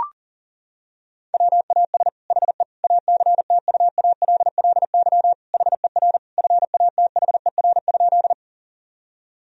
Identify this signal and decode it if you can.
Morse code, heard as a keyed tone, about 40 wpm, 690 Hz. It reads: WAS HE ACTUALLY HER FATHER?
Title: Sentences from Top 1000 Words - Review Part 4 - 40wpm